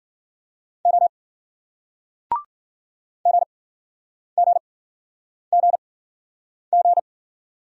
Morse code, heard as a keyed tone, about 51 wpm, spaced as for 9 wpm, 700 Hz. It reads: O GGGG